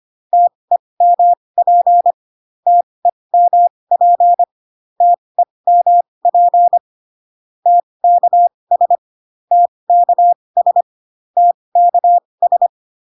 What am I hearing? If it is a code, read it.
Morse code, heard as a keyed tone, about 25 wpm, spaced as for 20 wpm, 700 Hz. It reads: TEMP TEMP TEMP TKS TKS TKS